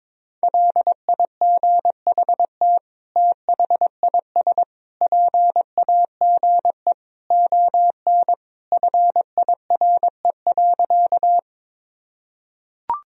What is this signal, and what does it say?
Morse code, heard as a keyed tone, about 22 wpm, 700 Hz. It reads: LIGHT THIS PAGE ON FIRE.
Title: Sentences from Top 1000 Words - Review Part 1 - 22wpm